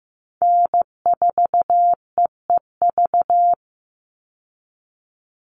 Morse code, heard as a keyed tone, about 15 wpm, 700 Hz. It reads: N4EEV
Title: Call Signs Repeated 3x - 15wpm